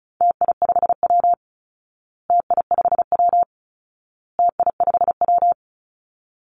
Morse code, heard as a keyed tone, about 34 wpm, 700 Hz. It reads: TI5W TI5W TI5W
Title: Call Signs Repeated 2x - 35wpm